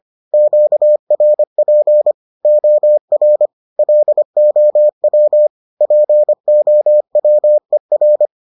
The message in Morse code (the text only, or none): QRP OR LOW POWER